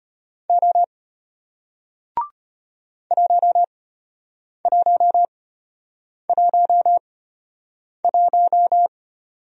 Morse code, heard as a keyed tone, about 33 wpm, spaced as for 8 wpm, 700 Hz. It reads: O 1111